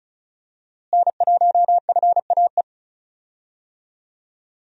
Morse code, heard as a keyed tone, about 34 wpm, 700 Hz. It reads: N1FAE
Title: USA Call Signs - 5 Characters - Repeated 3x - 35wpm